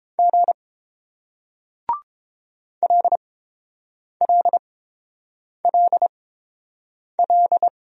Morse code, heard as a keyed tone, about 29 wpm, spaced as for 7 wpm, 700 Hz. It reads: G LLLL